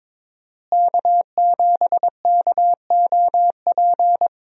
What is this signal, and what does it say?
Morse code, heard as a keyed tone, about 22 wpm, 700 Hz. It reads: K7KOP
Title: USA Call Signs - 5 Characters - Repeated 2x - 22wpm